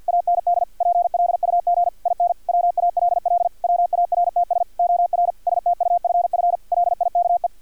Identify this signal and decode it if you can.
Morse code, heard as a keyed tone, about 50 wpm, 700 Hz, with radio noise. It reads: AND PLUG IN WALL WARTS OR STUFF LIKE
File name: archived / CRSnet